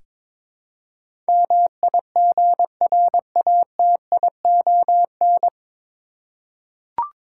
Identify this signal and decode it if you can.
Morse code, heard as a keyed tone, about 22 wpm, 700 Hz. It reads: MIGRATION
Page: Morse Code Ninja practice 22wpm